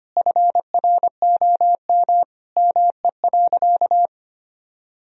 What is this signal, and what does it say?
Morse code, heard as a keyed tone, about 25 wpm, 700 Hz. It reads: FROM ME.
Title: Sentences from Top 1000 Words - Review Part 1 - 25wpm